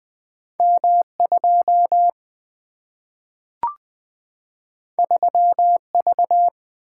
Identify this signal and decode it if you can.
Morse code, heard as a keyed tone, about 20 wpm, 700 Hz. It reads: M 2 3 V
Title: Single-Letter Single-Number 20wpm